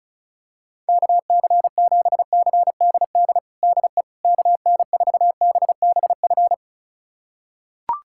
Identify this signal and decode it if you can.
Morse code, heard as a keyed tone, about 35 wpm, 700 Hz. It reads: KC7CDD DE KN4BBF
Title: USA Callsign DE Callsign - 6 Characters - 35wpm